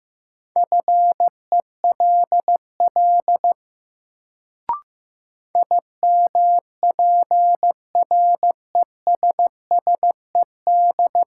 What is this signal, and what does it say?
Morse code, heard as a keyed tone, about 15 wpm, 700 Hz. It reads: FELL IMPRESSED
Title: Sentences from Top 1600 Words - New Word Drill - 15wpm